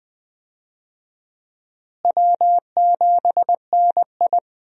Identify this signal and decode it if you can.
Morse code, heard as a keyed tone, about 20 wpm, 700 Hz. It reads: W7NI
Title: USA Call Signs - 4 Characters - Repeated 2x - 20wpm